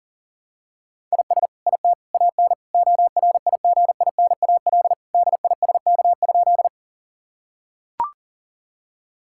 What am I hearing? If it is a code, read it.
Morse code, heard as a keyed tone, about 40 wpm, 700 Hz. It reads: IS IT AN ORIGINAL DISK?
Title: Sentences from Top 1000 Words - Review Part 6 - 40wpm